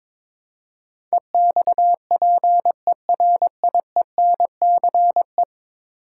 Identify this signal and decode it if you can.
Morse code, heard as a keyed tone, about 22 wpm, 700 Hz. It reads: EXPERIENCE
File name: practice / 22wpm / 10 Letter Words - First 3 Letters Given - 22wpm